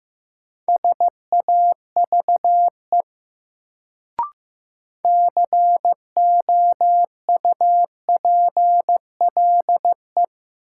Morse code, heard as a keyed tone, about 15 wpm, 700 Hz. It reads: SAVE COUPLE
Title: Sentences from Top 1100 Words - New Word Drill - 15wpm